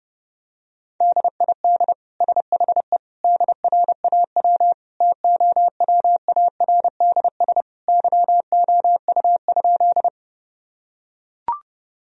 Morse code, heard as a keyed tone, about 30 wpm, 700 Hz. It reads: DID SHE DRAW TOWARDS YOU?
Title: Sentences from Top 1000 Words - Review Part 2 - 30wpm